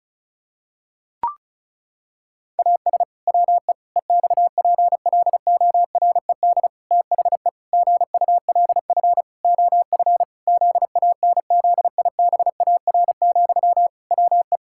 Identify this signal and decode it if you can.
Morse code, heard as a keyed tone, about 34 wpm, 700 Hz. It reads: AS WE EXPLORED THE GULF OF ZANZIBAR, WE
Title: Pangram Challenge v2 - 35wpm